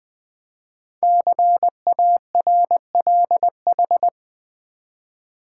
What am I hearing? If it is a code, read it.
Morse code, heard as a keyed tone, about 20 wpm, 700 Hz. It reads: CARLH
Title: Code Groups - 5 Letters - 20wpm